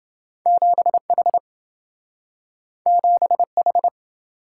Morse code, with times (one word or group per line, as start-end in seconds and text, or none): 0.5-1.4 7H
2.9-3.9 7H